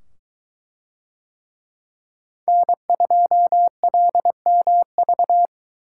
Morse code, heard as a keyed tone, about 22 wpm, 700 Hz. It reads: N2LMV